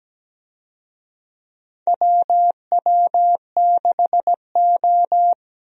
Morse code, heard as a keyed tone, about 17 wpm, 700 Hz. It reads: WW6O
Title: USA Call Signs - 4 Characters - 17wpm